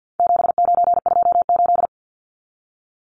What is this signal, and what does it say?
Morse code, heard as a keyed tone, about 50 wpm, 700 Hz. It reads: 7928
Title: Numbers - 4-Digit 50wpm